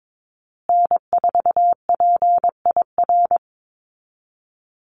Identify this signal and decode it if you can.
Morse code, heard as a keyed tone, about 22 wpm, 690 Hz. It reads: N4PIR
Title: Call Signs Repeated 2x - 22wpm